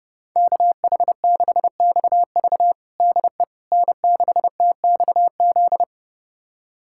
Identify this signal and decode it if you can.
Morse code, heard as a keyed tone, about 32 wpm, 700 Hz. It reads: KH6XV DE N6TXZ
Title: USA Callsign DE Callsign - 5 Characters - 30wpm